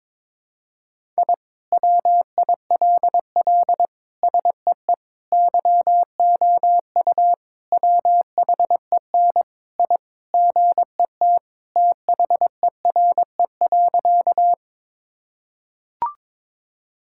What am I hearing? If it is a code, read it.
Morse code, heard as a keyed tone, about 22 wpm, 700 Hz. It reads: I WILL SEE YOU WHEN I GET THERE.
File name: practice / 22wpm / Sentences with 2-Word Phrases - 22wpm